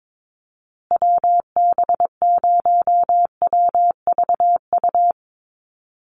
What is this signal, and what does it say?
Morse code, heard as a keyed tone, about 22 wpm, 700 Hz. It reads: WB0WVU
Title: USA Call Signs Repeated 2x 22wpm